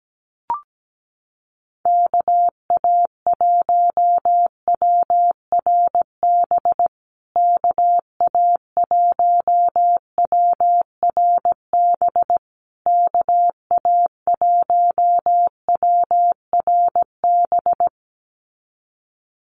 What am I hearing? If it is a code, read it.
Morse code, heard as a keyed tone, about 17 wpm, 700 Hz. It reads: KA1WRB KA1WRB KA1WRB
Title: USA Call Signs Repeated 3x 17wpm